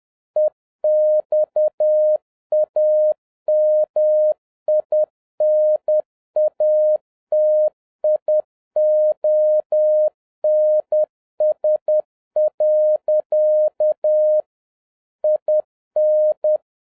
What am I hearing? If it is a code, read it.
Morse code, heard as a keyed tone, about 10 wpm, 610 Hz. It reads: EXAMINATIONS. IN